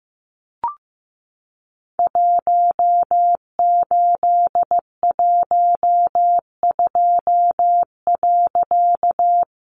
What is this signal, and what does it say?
Morse code, heard as a keyed tone, about 15 wpm, 700 Hz. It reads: 1812.
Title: A Sound Of Thunder 15wpm